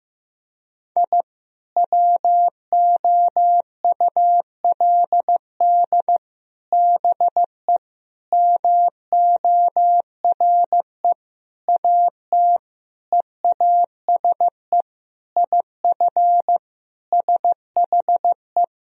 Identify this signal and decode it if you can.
Morse code, heard as a keyed tone, about 15 wpm, 700 Hz. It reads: I WOULD BE MORE AT EASE IF SHE
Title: Sentences from Top 1000 Words - Review Part 1 - 15wpm